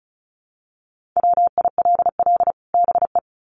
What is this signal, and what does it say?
Morse code, heard as a keyed tone, about 34 wpm, 700 Hz. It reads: WILL BE
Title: Sets of 2 Words 35wpm